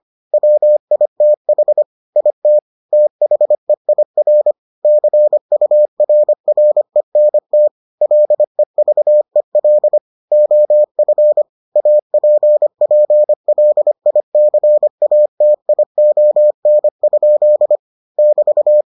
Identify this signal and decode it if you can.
Morse code, heard as a keyed tone, about 25 wpm, 600 Hz. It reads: WITH IT THEIR CURRENT LEVEL OF APPLICATION? <BT>